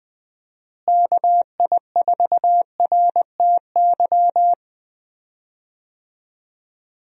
Morse code, heard as a keyed tone, about 21 wpm, 700 Hz. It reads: KI4RTY
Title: USA Call Signs - 6 Characters - Repeated 2x - 20wpm